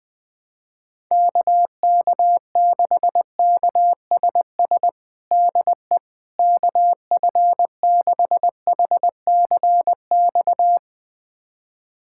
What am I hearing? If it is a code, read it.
Morse code, heard as a keyed tone, about 21 wpm, 700 Hz. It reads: KK6KSS DE KF6HCX
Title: USA Callsign DE Callsign - 6 Characters - 20wpm